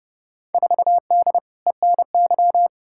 Morse code, heard as a keyed tone, about 29 wpm, 700 Hz. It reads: 4D ENY
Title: Field Day Exchanges Encore - 30wpm